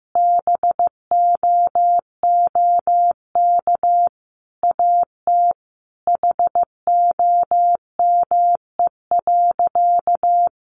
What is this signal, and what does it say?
Morse code, heard as a keyed tone, about 15 wpm, 700 Hz. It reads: BOOK AT HOME.
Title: Sentences from Top 1400 Words 15wpm